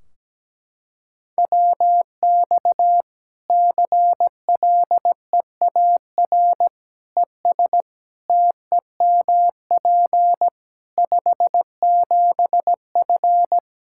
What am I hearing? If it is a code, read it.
Morse code, heard as a keyed tone, about 17 wpm, 700 Hz. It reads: WX CLEAR ES TEMP 57F
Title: Practice QSO - Exchange 2 - Info - 17wpm